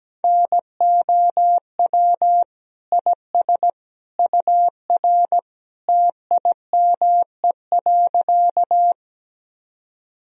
Morse code, heard as a keyed tone, about 17 wpm, 700 Hz. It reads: NOW IS UR TIME.
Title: Sentences from Top 100 Words and Abbreviations 17wpm